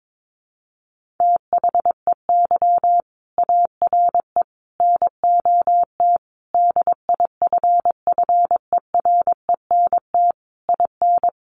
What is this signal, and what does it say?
Morse code, heard as a keyed tone, about 22 wpm, 700 Hz. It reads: THEY ARE NOT DIFFERENT IN
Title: Sentences from Top 1400 Words 22wpm